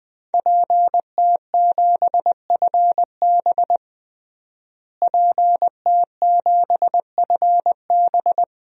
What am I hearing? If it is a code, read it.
Morse code, heard as a keyed tone, about 21 wpm, 700 Hz. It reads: PT7FB PT7FB
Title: DXCC-100 Call Signs - 5 Characters - Repeated 2x - 20wpm